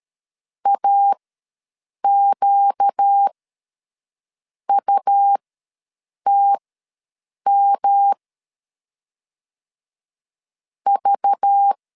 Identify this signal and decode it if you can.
Morse code, heard as a keyed tone, about 13 wpm, spaced as for 6 wpm, 800 Hz. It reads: AQUTM V
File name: LEZIONI / 6 PKX0 / 6 PKX0 ws 12wpm 3448